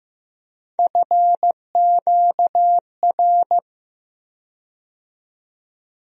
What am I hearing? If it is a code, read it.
Morse code, heard as a keyed tone, about 15 wpm, 700 Hz. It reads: FQR
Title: USA Call Sign Postfixes - 3 Letters 15wpm